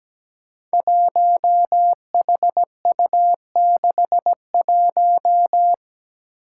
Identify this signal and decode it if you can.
Morse code, heard as a keyed tone, about 17 wpm, 700 Hz. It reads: 1HU61
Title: Five Challenging Letter-Number Combinations 17wpm